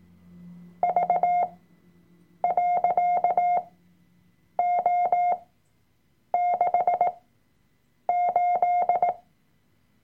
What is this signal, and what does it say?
Morse code, heard as a keyed tone, about 18 wpm, spaced as for 7 wpm, 710 Hz. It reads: V.O68
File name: Workshop Dateien / CW-Online-Kurs / Lektion 003